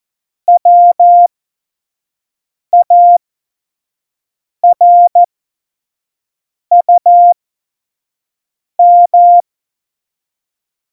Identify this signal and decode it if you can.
Morse code, heard as a keyed tone, about 14 wpm, spaced as for 5 wpm, 700 Hz. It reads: WARUM
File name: Morsekurs / Witze / Witz 18 August 2020, 25 F70